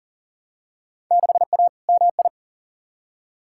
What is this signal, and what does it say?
Morse code, heard as a keyed tone, about 39 wpm, 700 Hz. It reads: 6A MI
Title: Field Day Exchanges - 40wpm